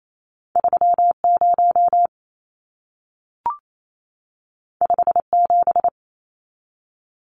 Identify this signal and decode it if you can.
Morse code, heard as a keyed tone, about 28 wpm, 700 Hz. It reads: 30 57
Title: Numbers - 2-Digit 28wpm